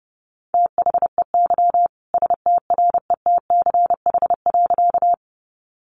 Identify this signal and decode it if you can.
Morse code, heard as a keyed tone, about 30 wpm, 700 Hz. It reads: THEY STRETCH.
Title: Easy Phrases 30wpm